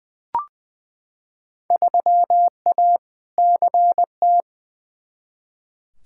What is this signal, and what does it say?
Morse code, heard as a keyed tone, about 21 wpm, 700 Hz. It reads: 3A CT
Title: Field Day Exchanges - 20wpm